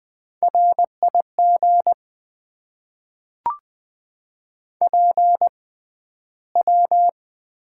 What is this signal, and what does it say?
Morse code, heard as a keyed tone, about 20 wpm, spaced as for 12 wpm, 700 Hz. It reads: RIG PW
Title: Top 90 QSO Elements - 6x Character Spacing - 20wpm